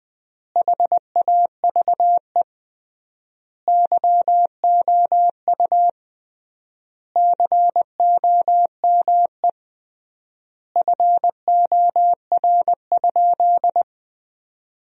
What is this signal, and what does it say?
Morse code, heard as a keyed tone, about 21 wpm, 700 Hz. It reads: HAVE YOU COME FOR?
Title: 3x Word-Spacing - Sentences from Top 100 Words - 20wpm